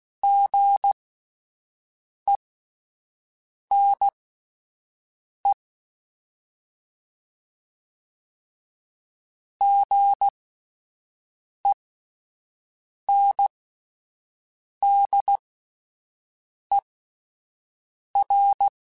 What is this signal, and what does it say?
Morse code, heard as a keyed tone, about 16 wpm, spaced as for 5 wpm, 800 Hz. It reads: GENE GENDER
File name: Alphabetical / Alphabetical mp3s / Gs @5